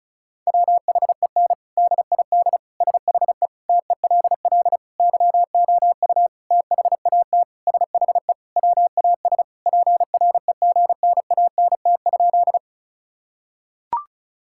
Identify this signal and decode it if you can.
Morse code, heard as a keyed tone, about 34 wpm, 700 Hz. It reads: WHEN DID SHE TELL YOU THAT SHE WAS PREGNANT?
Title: Sentences from Top 1000 Words - Review Part 6 - 35wpm